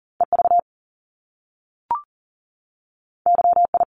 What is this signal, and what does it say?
Morse code, heard as a keyed tone, about 39 wpm, 700 Hz. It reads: EV YI
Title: Top Two Letter Combinations in Words 40wpm